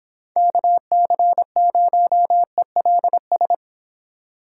Morse code, heard as a keyed tone, about 26 wpm, 700 Hz. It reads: KC0ELS